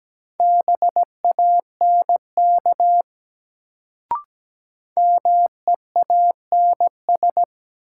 Morse code, heard as a keyed tone, about 17 wpm, 700 Hz. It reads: BANK MEANS
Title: Sets of 5 Words - New Word Drill - 17wpm